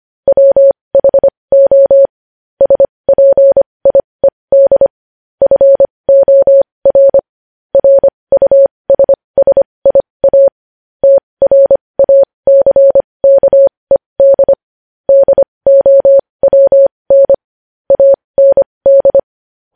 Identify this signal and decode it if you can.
Morse code, heard as a keyed tone, about 25 wpm, spaced as for 20 wpm, 550 Hz. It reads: WHO SPIED FOR RUSSIA TRACKED DOWN AND
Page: news-headlines-in-morse-code-at-20-wpm